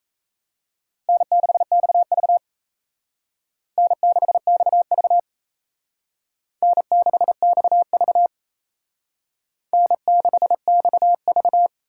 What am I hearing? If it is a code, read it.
Morse code, heard as a keyed tone, about 36 wpm, 700 Hz. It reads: N6XV N6XV N6XV N6XV